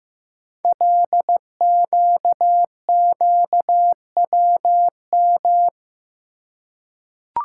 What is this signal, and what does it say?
Morse code, heard as a keyed tone, about 15 wpm, 700 Hz. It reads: LQQWM
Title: Code Groups - 5 Letters - 15wpm